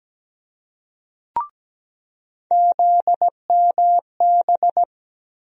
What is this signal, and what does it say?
Morse code, heard as a keyed tone, about 17 wpm, 700 Hz. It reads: ZMB
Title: USA Call Sign Postfixes - 3 Letters 17wpm